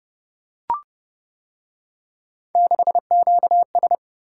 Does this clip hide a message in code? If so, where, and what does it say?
Yes: Morse code from 2.6 to 4.0 s, heard as a keyed tone, about 28 wpm, 700 Hz. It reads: 6QS